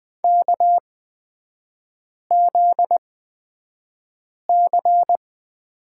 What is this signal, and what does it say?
Morse code, heard as a keyed tone, about 20 wpm, spaced as for 5 wpm, 700 Hz. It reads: KZC
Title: 243 Lesson 39e - Letter z - Three Characters 20wpm